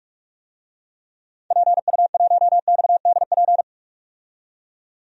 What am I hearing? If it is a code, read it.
Morse code, heard as a keyed tone, about 42 wpm, 700 Hz. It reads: PU1XDP
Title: DXCC-100 Call Signs - 6 Characters - 45wpm